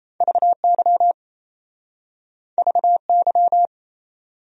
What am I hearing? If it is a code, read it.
Morse code, heard as a keyed tone, about 37 wpm, 700 Hz. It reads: VY VY